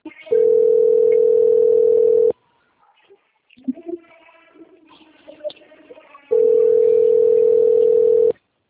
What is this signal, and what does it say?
Ringback tone (North America), 440 + 480 Hz.
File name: out- - - - - .34303